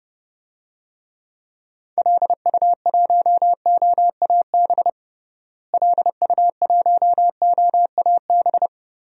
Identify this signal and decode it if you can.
Morse code, heard as a keyed tone, about 30 wpm, 700 Hz. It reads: LU1OAB LU1OAB
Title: DXCC-100 Call Signs - 6 Characters - Repeated 3x - 30wpm